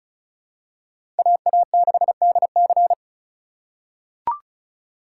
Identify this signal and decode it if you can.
Morse code, heard as a keyed tone, about 36 wpm, 700 Hz. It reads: AA6DC
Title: USA Call Signs - 5 Characters - 2x Character Spacing - 35wpm